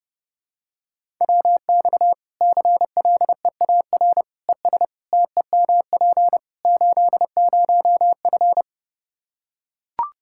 Morse code, heard as a keyed tone, about 32 wpm, 700 Hz. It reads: WX CLEAR ES TEMP 80F
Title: K1USN SST QSO Practice 30wpm